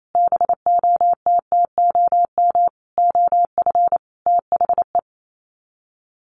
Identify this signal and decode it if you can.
Morse code, heard as a keyed tone, about 28 wpm, 700 Hz. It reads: BOTTOM OF THE
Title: Sets of 5 Words Encore2 28wpm